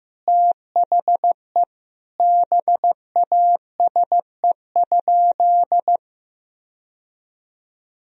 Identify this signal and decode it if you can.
Morse code, heard as a keyed tone, about 15 wpm, 700 Hz. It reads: THE BASE?
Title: Sentences from Top 1000 Words - Review Part 1 - 15wpm